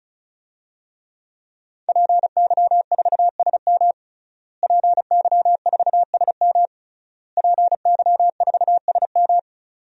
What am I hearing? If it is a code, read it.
Morse code, heard as a keyed tone, about 36 wpm, 700 Hz. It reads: PY4SM PY4SM PY4SM
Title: DXCC-100 Call Signs - 5 Characters - Repeated 3x - 35wpm